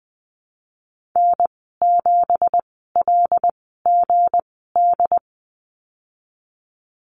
Morse code, heard as a keyed tone, about 20 wpm, spaced as for 15 wpm, 700 Hz. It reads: N7LGD
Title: 179 Lesson 29g - Slash - Call Signs 20wpm